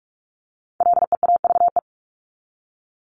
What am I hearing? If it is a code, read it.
Morse code, heard as a keyed tone, about 44 wpm, 700 Hz. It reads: LEAVE
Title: ING Suffix - Top 100 Words with Base Words 45wpm